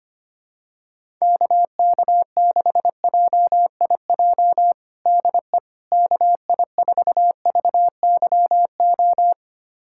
Morse code, heard as a keyed tone, about 25 wpm, 700 Hz. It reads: KK6JIJ DE KI4VYO
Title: USA Callsign DE Callsign - 6 Characters - 25wpm